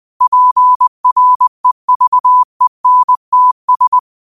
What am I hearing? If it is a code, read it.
Morse code, heard as a keyed tone, about 20 wpm, 1000 Hz. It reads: PREVENTS